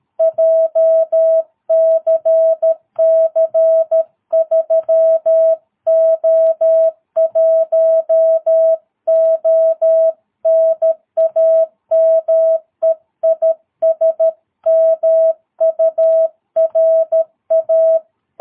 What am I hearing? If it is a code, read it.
Morse code, heard as a keyed tone, about 13 wpm, 650 Hz. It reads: JCC3O1O NAME IS MURA
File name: jh8cbh / 2022cw